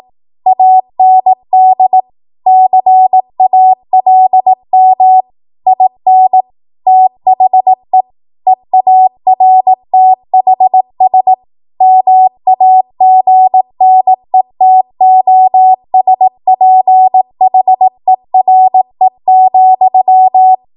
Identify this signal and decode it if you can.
Morse code, heard as a keyed tone, about 18 wpm, 750 Hz. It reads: AND CALM IN THE EARTHS MAGNETOSPHERE,
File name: Archive / 18 WPM / 18WPM